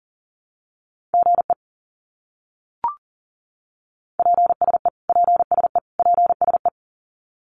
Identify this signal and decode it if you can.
Morse code, heard as a keyed tone, about 40 wpm, 700 Hz. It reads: GE PSE PSE PSE
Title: Abbreviations Repeated 3x - 40wpm